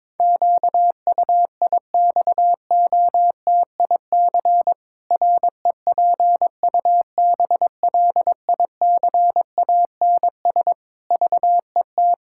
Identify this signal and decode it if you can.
Morse code, heard as a keyed tone, about 22 wpm, 700 Hz. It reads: QUIXOTIC REPUBLICANS VET
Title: Pangram Challenge v2 - 22wpm